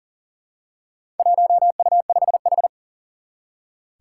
Morse code, heard as a keyed tone, about 37 wpm, 700 Hz. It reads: 1U5H
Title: Four Challenging Letter-Number Combinations 40wpm